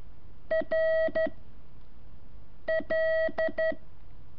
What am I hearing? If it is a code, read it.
Morse code, heard as a keyed tone, about 13 wpm, 640 Hz. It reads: R L
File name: curso / ejercicio 3